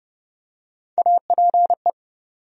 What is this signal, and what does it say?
Morse code, heard as a keyed tone, about 30 wpm, 700 Hz. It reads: APE
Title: USA Call Sign Postfixes - 3 Letters 30wpm